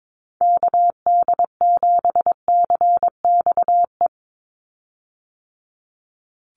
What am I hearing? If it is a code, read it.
Morse code, heard as a keyed tone, about 22 wpm, 700 Hz. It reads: KD7CXE
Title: USA Call Signs 22wpm